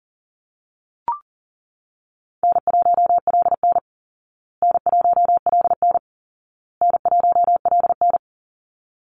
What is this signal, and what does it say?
Morse code, heard as a keyed tone, about 40 wpm, 700 Hz. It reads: N1LN N1LN N1LN
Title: Easy Call Signs Repeated 3x - 40wpm